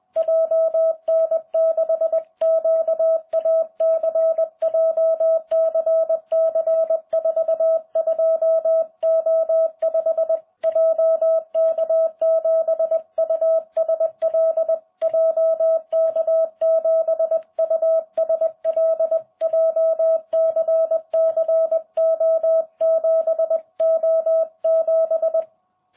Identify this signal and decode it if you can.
Morse code, heard as a keyed tone, about 20 wpm, 650 Hz. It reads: JN6QAC JCC42O5 JK7USL JK7USL JCCO7O7